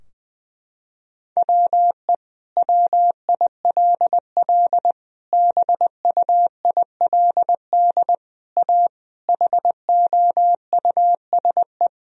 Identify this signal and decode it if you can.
Morse code, heard as a keyed tone, about 20 wpm, 700 Hz. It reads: WE WILL BUILD A HOUSE
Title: Sentences from Top 1000 Words - Review Part 4 - 20wpm